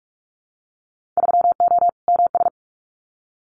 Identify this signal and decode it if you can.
Morse code, heard as a keyed tone, about 46 wpm, 700 Hz. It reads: 3O MS
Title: Winter Field Day Exchanges Encore 45wpm